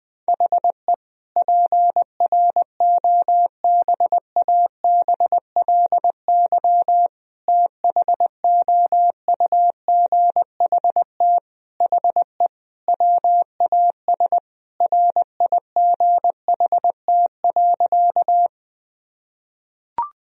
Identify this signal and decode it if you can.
Morse code, heard as a keyed tone, about 20 wpm, 700 Hz. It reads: HE PROBABLY THOUGHT HE WAS RIGHT.
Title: Sentences from Top 1000 Words - Review Part 4 - 20wpm